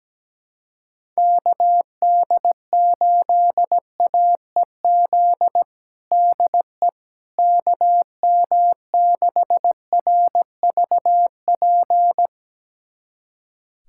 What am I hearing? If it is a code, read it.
Morse code, heard as a keyed tone, about 17 wpm, 700 Hz. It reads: KD8AEZ DE KM6RVP
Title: USA Callsign DE Callsign - 6 Characters - 17wpm